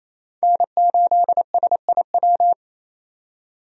Morse code, heard as a keyed tone, about 28 wpm, 700 Hz. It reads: N8SIW
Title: USA Call Signs - 5 Characters - 28wpm